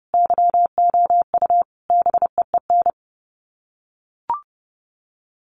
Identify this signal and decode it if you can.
Morse code, heard as a keyed tone, about 30 wpm, 700 Hz. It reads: YOU BEEN
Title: Sets of 5 Words Encore 30wpm